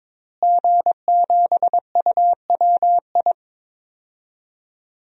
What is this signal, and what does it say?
Morse code, heard as a keyed tone, about 22 wpm, 700 Hz. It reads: G7UWI
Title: DXCC-100 Call Signs - 5 Characters - Repeated 3x - 22wpm